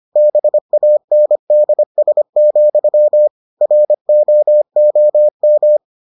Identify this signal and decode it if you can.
Morse code, heard as a keyed tone, about 25 wpm, 600 Hz. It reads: BANDS, ROOM